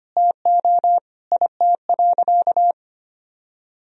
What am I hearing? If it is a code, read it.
Morse code, heard as a keyed tone, about 25 wpm, 700 Hz. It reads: TO IT.
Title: Sentences from Top 1000 Words - Review Part 2 - 25wpm